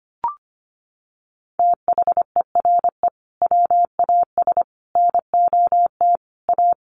Morse code, heard as a keyed tone, about 25 wpm, 700 Hz. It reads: THERE WAS NOT A
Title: Sentences from Top 1500 Words 25wpm